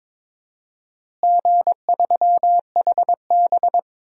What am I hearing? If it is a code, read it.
Morse code, heard as a keyed tone, about 22 wpm, 700 Hz. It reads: G3HB
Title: DXCC-100 Call Signs - 4 Characters - Repeated 3x - 22wpm